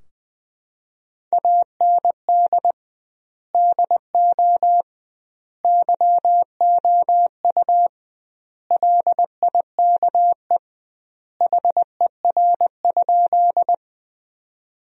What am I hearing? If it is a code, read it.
Morse code, heard as a keyed tone, about 20 wpm, 700 Hz. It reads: AND DO YOU LIKE HER?
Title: 2x Word-Spacing - Sentences from Top 100 Words - 20wpm